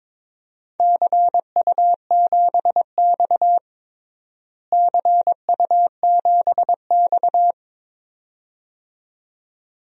Morse code, heard as a keyed tone, about 22 wpm, 700 Hz. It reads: CU7X CU7X
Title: DXCC-100 Call Signs - 4 Characters - Repeated 2x - 22wpm